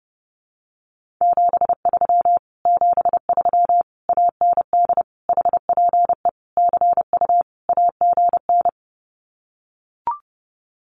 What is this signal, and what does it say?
Morse code, heard as a keyed tone, about 30 wpm, 700 Hz. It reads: 73 73 AND HPE CU AGN
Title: Common QSO Phrases Encore 30wpm